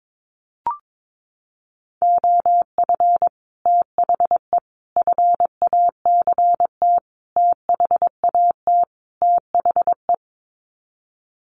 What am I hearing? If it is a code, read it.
Morse code, heard as a keyed tone, about 22 wpm, 700 Hz. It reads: OF THE FACT THAT THE
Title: Sets of 5 Words Encore2 22wpm